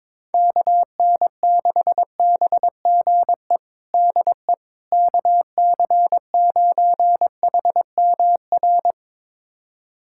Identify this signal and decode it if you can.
Morse code, heard as a keyed tone, about 22 wpm, 700 Hz. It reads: KN6BGE DE KC9HMR
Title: USA Callsign DE Callsign 22wpm